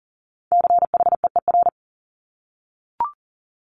Morse code, heard as a keyed tone, about 42 wpm, 700 Hz. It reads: CHEER
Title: Multi-letter Phonemes Repeated 3x 40wpm